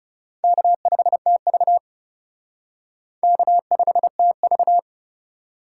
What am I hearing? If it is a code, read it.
Morse code, heard as a keyed tone, about 37 wpm, 700 Hz. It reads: K5TV K5TV
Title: Speed Racing - USA Call Signs 30wpm